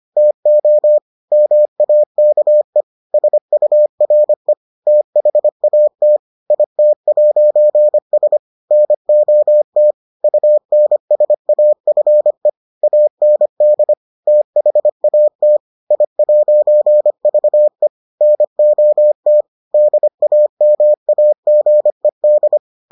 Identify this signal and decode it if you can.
Morse code, heard as a keyed tone, about 25 wpm, 600 Hz. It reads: TO MAKE SURE THAT IT'S NOT UNSAFE AND THAT I'VE NOT DAMAGED